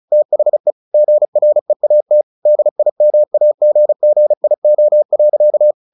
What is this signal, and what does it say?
Morse code, heard as a keyed tone, about 34 wpm, 600 Hz. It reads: THE GREAT DIMAGGIO.
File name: the old man and the sea 2 35wpm